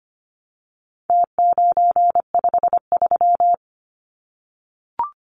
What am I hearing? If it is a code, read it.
Morse code, heard as a keyed tone, about 25 wpm, 700 Hz. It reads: T953
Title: Numbers - Contest Style - 4-Digit 25wpm